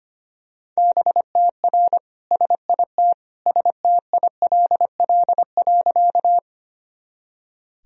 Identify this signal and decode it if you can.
Morse code, heard as a keyed tone, about 25 wpm, 700 Hz. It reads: BTR SIT STILL.
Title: Sentences from Top 500 Words and Abbreviations 25wpm